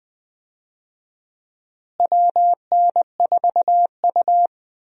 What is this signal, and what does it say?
Morse code, heard as a keyed tone, about 20 wpm, 700 Hz. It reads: WN4U
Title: USA Call Signs - 4 Characters - Repeated 2x - 20wpm